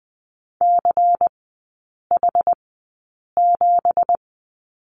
Morse code, heard as a keyed tone, about 20 wpm, spaced as for 8 wpm, 700 Hz. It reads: CH7
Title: 198 Lesson 32e - Letter k - Three Characters 20wpm